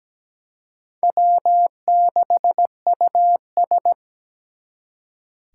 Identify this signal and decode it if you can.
Morse code, heard as a keyed tone, about 20 wpm, 700 Hz. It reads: W6US
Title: Speed Racing - USA Call Signs 17wpm